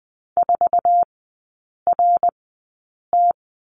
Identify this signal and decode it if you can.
Morse code, heard as a keyed tone, about 20 wpm, spaced as for 8 wpm, 700 Hz. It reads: 4RT
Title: 074 Lesson 13e - Letter L - Three Characters 20wpm